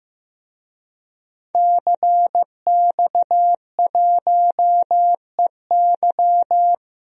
Morse code, heard as a keyed tone, about 16 wpm, 700 Hz. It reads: CX1EY
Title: DXCC-100 Call Signs - 5 Characters - Repeated 3x - 15wpm